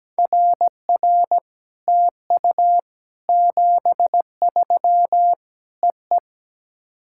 Morse code, heard as a keyed tone, about 17 wpm, 700 Hz. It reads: RR TU 73 EE
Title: SOTA QSO Practice 17wpm